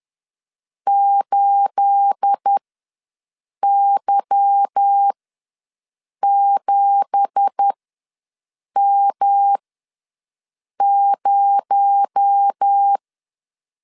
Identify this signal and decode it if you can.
Morse code, heard as a keyed tone, about 10 wpm, spaced as for 6 wpm, 800 Hz. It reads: 8Y7M0
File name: LEZIONI / 9 2867 ws 10wpm 4215